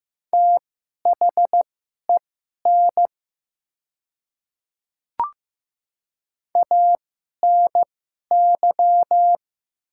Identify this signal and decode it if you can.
Morse code, heard as a keyed tone, about 15 wpm, spaced as for 11 wpm, 700 Hz. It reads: THEN ANY
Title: 2x Character Spacing - Top 200 Words - 15wpm